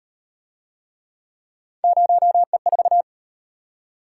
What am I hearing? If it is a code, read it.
Morse code, heard as a keyed tone, about 33 wpm, 700 Hz. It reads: 0E4